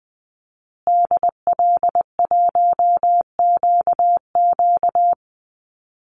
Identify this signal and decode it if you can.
Morse code, heard as a keyed tone, about 19 wpm, 700 Hz. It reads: DL1QQ
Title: Call Signs Repeated 2x - 20wpm